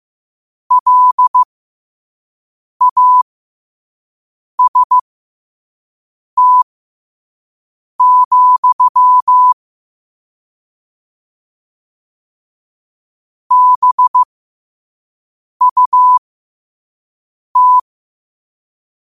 Morse code, heard as a keyed tone, about 15 wpm, spaced as for 5 wpm, 1000 Hz. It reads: LAST, BUT
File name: 5wpm